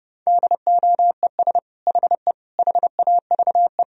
Morse code, heard as a keyed tone, about 30 wpm, 700 Hz. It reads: DOES HE HAVE
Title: Sentences from Top 1000 Words - Review Part 3 - 30wpm